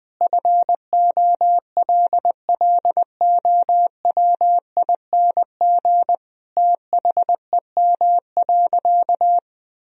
Morse code, heard as a keyed tone, about 20 wpm, 700 Hz. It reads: FOLLOWING THEM.
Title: Sentences from Top 1000 Words - Review Part 4 - 20wpm